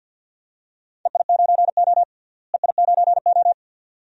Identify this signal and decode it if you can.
Morse code, heard as a keyed tone, about 50 wpm, 700 Hz. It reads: EI9O EI9O
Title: DXCC-100 Call Signs - 4 Characters - Repeated 2x - 50wpm